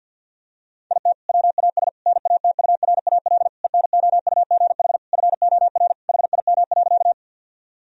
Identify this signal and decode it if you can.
Morse code, heard as a keyed tone, about 49 wpm, 700 Hz. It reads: IT WAS NATURAL ENOUGH FOR HIM.
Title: Sentences from Top 1000 Words - Review Part 4 - 50wpm